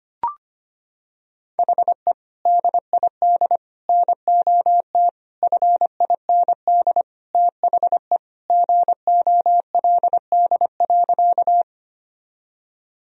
Morse code, heard as a keyed tone, about 25 wpm, 700 Hz. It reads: HE DID NOT FIND THE GOLD.
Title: Sentences from Top 1000 Words - Review Part 3 - 25wpm